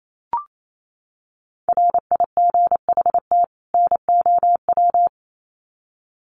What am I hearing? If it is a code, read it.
Morse code, heard as a keyed tone, about 28 wpm, 700 Hz. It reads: RIGHT NOW
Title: Sets of 2 Words 28wpm